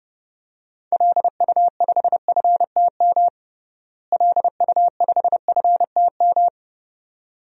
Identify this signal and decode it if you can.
Morse code, heard as a keyed tone, about 32 wpm, 700 Hz. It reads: LU5FTM LU5FTM
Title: DXCC-100 Call Signs - 6 Characters - Repeated 2x - 30wpm